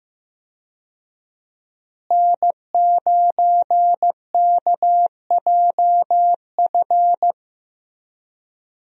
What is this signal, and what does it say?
Morse code, heard as a keyed tone, about 16 wpm, 700 Hz. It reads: N9KJF
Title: USA Call Signs - 5 Characters - Repeated 3x - 15wpm